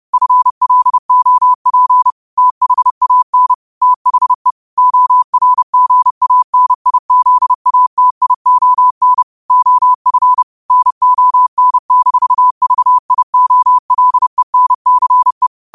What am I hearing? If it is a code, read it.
Morse code, heard as a keyed tone, about 30 wpm, 1000 Hz. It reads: FLOP THAN THE ORGANIZATION OF NON-VIOLENCE
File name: qotdfiles / 30wpm